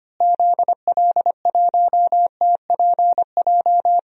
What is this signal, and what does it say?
Morse code, heard as a keyed tone, about 25 wpm, 700 Hz. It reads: ZL1TPJ
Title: DXCC-100 Call Signs - 6 Characters - Repeated 3x - 25wpm